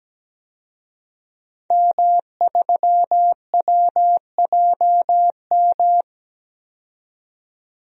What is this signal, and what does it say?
Morse code, heard as a keyed tone, about 17 wpm, 700 Hz. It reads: M3WJM
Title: DXCC-100 Call Signs - 5 Characters - 17wpm